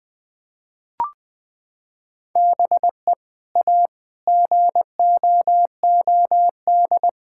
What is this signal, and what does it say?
Morse code, heard as a keyed tone, about 19 wpm, 700 Hz. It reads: BE A GOOD
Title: Sentences with 2-Word Phrases - 20wpm